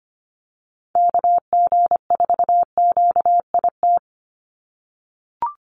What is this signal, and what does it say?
Morse code, heard as a keyed tone, about 25 wpm, 700 Hz. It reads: KG4QIT
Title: USA Call Signs 25wpm